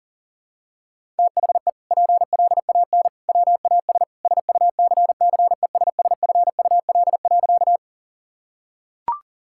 Morse code, heard as a keyed tone, about 40 wpm, 700 Hz. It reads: THE PLAN WAS SUCCESSFUL.
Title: Sentences from Top 1000 Words - Review Part 6 - 40wpm